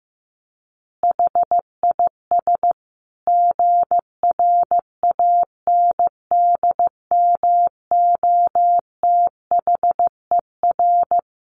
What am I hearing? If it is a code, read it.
Morse code, heard as a keyed tone, about 15 wpm, 700 Hz. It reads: HIS GRANDMOTHER
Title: Additional Easy Phrases 15wpm